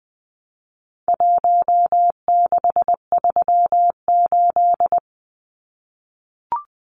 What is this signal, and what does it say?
Morse code, heard as a keyed tone, about 20 wpm, 700 Hz. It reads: 1638
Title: Numbers - Contest Style - 4-Digit 20wpm